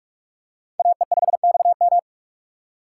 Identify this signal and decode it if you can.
Morse code, heard as a keyed tone, about 41 wpm, 700 Hz. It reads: AE5XM
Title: USA Call Signs - 5 Characters - 2x Character Spacing - 45wpm